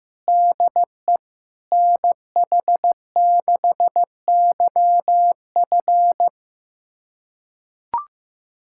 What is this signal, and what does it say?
Morse code, heard as a keyed tone, about 15 wpm, 700 Hz. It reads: DE NH6YF
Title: USA Callsign DE Callsign - 5 Characters - 15wpm